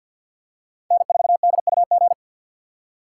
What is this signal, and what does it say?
Morse code, heard as a keyed tone, about 52 wpm, 700 Hz. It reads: N4DUG
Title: USA Call Signs - 4x Character Spacing - 50wpm